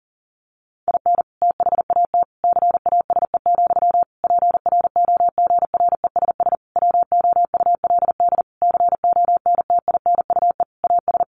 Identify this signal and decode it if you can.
Morse code, heard as a keyed tone, about 40 wpm, 700 Hz. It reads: IN THAT CASE, PROGRESS WOULD CONTINUE AS
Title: Sentences from Top 1400 Words 40wpm